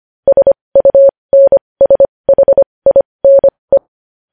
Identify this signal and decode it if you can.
Morse code, heard as a keyed tone, about 25 wpm, spaced as for 20 wpm, 550 Hz. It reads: SUNSHINE
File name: podcast / News headlines-1x-25x20